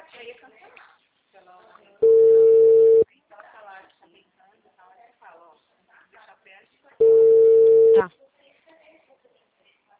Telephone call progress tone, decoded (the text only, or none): ringback tone